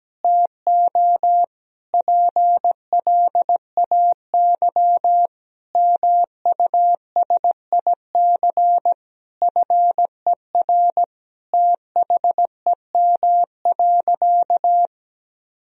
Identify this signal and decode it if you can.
Morse code, heard as a keyed tone, about 17 wpm, 700 Hz. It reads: TO PLAY MUSIC FER THEM.
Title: Sentences from Top 300 Words and Abbreviations 17wpm